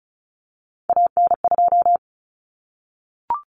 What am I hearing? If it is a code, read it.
Morse code, heard as a keyed tone, about 34 wpm, 700 Hz. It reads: AN2
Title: Common Cut Numbers - 3-Digit 35wpm